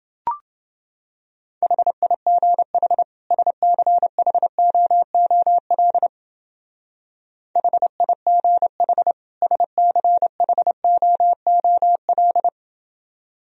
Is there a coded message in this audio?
Yes: Morse code from 1.6 to 12.5 s, heard as a keyed tone, about 28 wpm, 700 Hz. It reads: HIGH SCHOOL HIGH SCHOOL